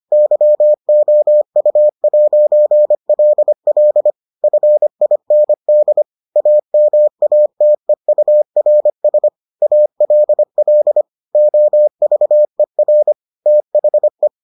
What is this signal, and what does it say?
Morse code, heard as a keyed tone, about 25 wpm, 600 Hz. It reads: YOU'LL FIND AMATEURS ALL OVER THE